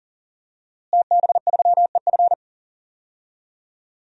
Morse code, heard as a keyed tone, about 40 wpm, 700 Hz. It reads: TB3EF
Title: DXCC-100 Call Signs - 5 Characters - Repeated 3x - 40wpm